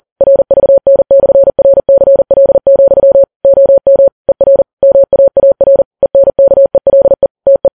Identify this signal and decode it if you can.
Morse code, heard as a keyed tone, about 40 wpm, 550 Hz. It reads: RVN/RCL, OM ER MAAR ENKELE TE
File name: Files Web tls / Morse on9fv / morse40kps